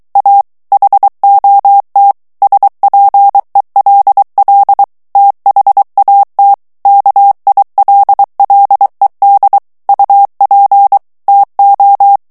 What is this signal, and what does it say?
Morse code, heard as a keyed tone, about 23 wpm, 800 Hz. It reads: A HOT SPELL THAT KILLED UP TO
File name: morse mp3 / ANGLAIS texte clair 2 vitesse 22 mots minute